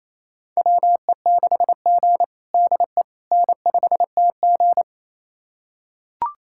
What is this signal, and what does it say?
Morse code, heard as a keyed tone, about 27 wpm, 700 Hz. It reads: WE6G DE N5TG
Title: USA Callsign DE Callsign - 4 Characters - 28wpm